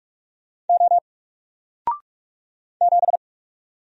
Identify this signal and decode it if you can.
Morse code, heard as a keyed tone, about 44 wpm, 700 Hz. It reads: O 7
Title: Single Character (Letter:Number:Punctuation:Prosign) 45wpm